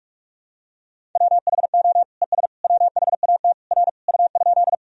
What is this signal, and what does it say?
Morse code, heard as a keyed tone, about 46 wpm, 700 Hz. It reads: WHO ES WHAT R U?
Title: Sentences from Top 100 Words and Abbreviations 45wpm